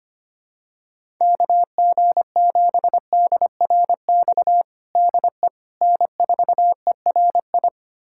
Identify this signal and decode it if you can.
Morse code, heard as a keyed tone, about 25 wpm, 700 Hz. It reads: KG7DRX DE N4ERI
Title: USA Callsign DE Callsign 25wpm